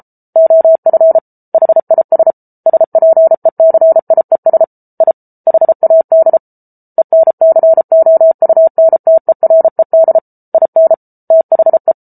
Morse code, heard as a keyed tone, about 34 wpm, 650 Hz. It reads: OF HIS SPECIES I HAD ENCOUNTERED IN THE